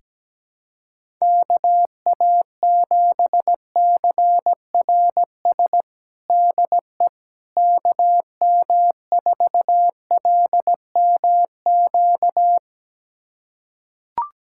Morse code, heard as a keyed tone, about 17 wpm, 700 Hz. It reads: KA7CRS DE KM4LMQ
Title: USA Callsign DE Callsign 17wpm